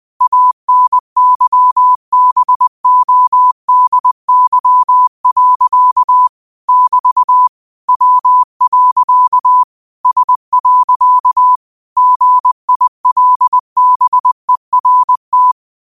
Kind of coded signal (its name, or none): Morse code